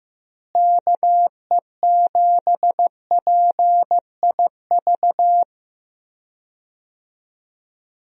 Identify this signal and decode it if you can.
Morse code, heard as a keyed tone, about 15 wpm, 700 Hz. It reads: KE7PIV
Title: USA Call Signs - 6 Characters - Repeated 3x - 15wpm